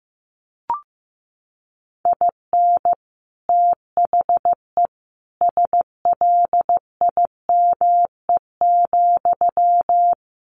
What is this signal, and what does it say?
Morse code, heard as a keyed tone, about 15 wpm, 700 Hz. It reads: IN THE SLIME,
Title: A Sound Of Thunder 15wpm